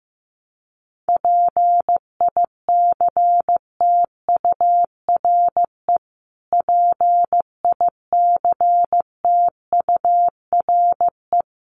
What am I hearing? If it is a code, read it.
Morse code, heard as a keyed tone, about 15 wpm, 700 Hz. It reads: PICTURE PICTURE
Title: Top 500 Words Repeated 15wpm